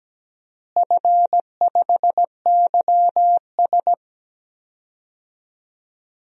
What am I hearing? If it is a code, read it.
Morse code, heard as a keyed tone, about 17 wpm, 700 Hz. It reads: F5YS
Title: DXCC-100 Call Signs - 4 Characters - Repeated 2x - 17wpm